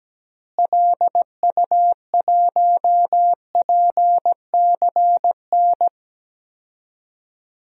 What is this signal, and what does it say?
Morse code, heard as a keyed tone, about 17 wpm, 700 Hz. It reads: LU1PCN
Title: DXCC-100 Call Signs - 6 Characters - Repeated 3x - 17wpm